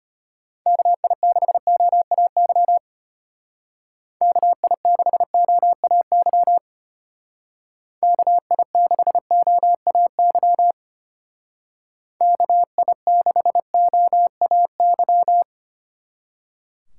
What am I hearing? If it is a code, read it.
Morse code, heard as a keyed tone, about 30 wpm, 700 Hz. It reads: KI6OAY KI6OAY KI6OAY KI6OAY
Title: Speed Racing - USA 2x3 Call Signs 25wpm